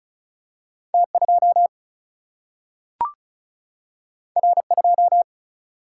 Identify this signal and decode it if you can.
Morse code, heard as a keyed tone, about 36 wpm, 700 Hz. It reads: T2 R2
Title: Single-Letter Single-Number 35wpm